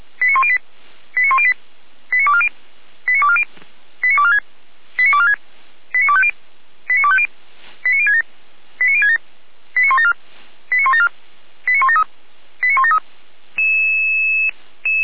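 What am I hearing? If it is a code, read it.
Five-tone selective call (ZVEI), heard as quick runs of five tones, tones of about 70 ms. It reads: 89108 89108 89240 89240 89246 89246 89250 89250 89067 89067 89174 89174 89173 89173
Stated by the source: Five-tone ZVEI selective call, address 89337, received on an SDR by the Orlamünde fire brigade.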